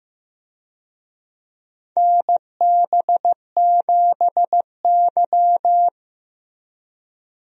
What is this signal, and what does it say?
Morse code, heard as a keyed tone, about 15 wpm, 700 Hz. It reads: NB7Y
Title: USA Call Signs - 4 Characters - Repeated 2x - 15wpm